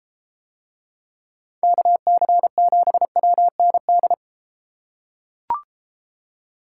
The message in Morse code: KC7WND

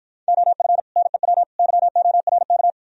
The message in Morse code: OF NEW YORK

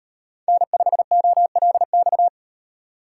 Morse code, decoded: N5OLX

